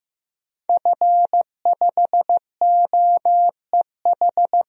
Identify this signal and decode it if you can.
Morse code, heard as a keyed tone, about 16 wpm, 700 Hz. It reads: F5OEH
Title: DXCC-100 Call Signs - 5 Characters - 15wpm